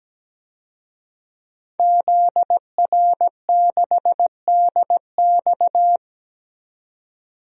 Morse code, heard as a keyed tone, about 17 wpm, 700 Hz. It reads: ZR6DX
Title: DXCC-100 Call Signs - 5 Characters - 17wpm